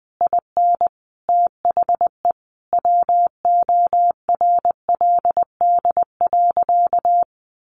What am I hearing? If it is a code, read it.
Morse code, heard as a keyed tone, about 20 wpm, 700 Hz. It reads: IN THE WORLD.
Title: Sentences from Top 1300 Words 20wpm